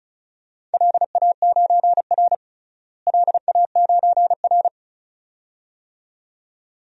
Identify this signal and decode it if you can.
Morse code, heard as a keyed tone, about 36 wpm, 700 Hz. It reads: LA9R LA9R
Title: DXCC-100 Call Signs - 4 Characters - Repeated 2x - 35wpm